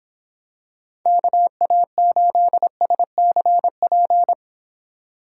Morse code, heard as a keyed tone, about 26 wpm, 700 Hz. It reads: KA8SCP